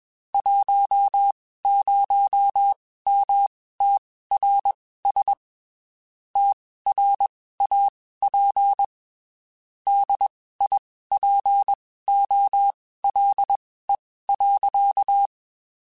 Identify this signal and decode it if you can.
Morse code, heard as a keyed tone, about 21 wpm, spaced as for 15 wpm, 800 Hz. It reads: 10MTRS TRAP DIPOLE.
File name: QSOs / QSOs-mp3s / WD0FBT 1@20